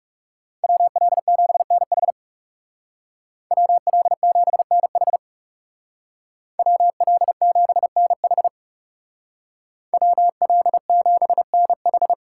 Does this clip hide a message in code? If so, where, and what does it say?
Yes: Morse code from 0.6 to 12.1 s, heard as a keyed tone, about 39 wpm, 700 Hz. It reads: WL7NH WL7NH WL7NH WL7NH